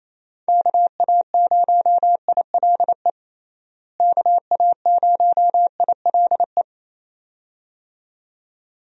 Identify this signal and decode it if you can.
Morse code, heard as a keyed tone, about 27 wpm, 700 Hz. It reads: KA0ILE KA0ILE
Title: USA Call Signs - 6 Characters - Repeated 2x - 28wpm